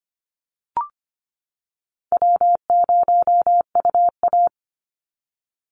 Morse code, heard as a keyed tone, about 25 wpm, 700 Hz. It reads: W0UA
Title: Easy Call Signs - 25wpm